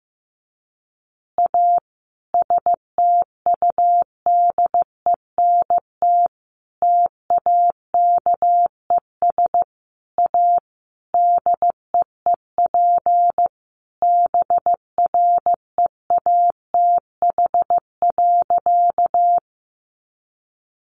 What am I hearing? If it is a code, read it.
Morse code, heard as a keyed tone, about 14 wpm, 700 Hz. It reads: A STUDENT TAKES A DEEP BREATH.
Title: Additional Easy Phrases 15wpm